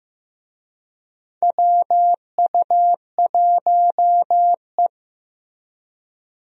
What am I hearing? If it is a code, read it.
Morse code, heard as a keyed tone, about 16 wpm, 700 Hz. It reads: WU1E
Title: USA Call Signs - 4 Characters - Repeated 3x - 15wpm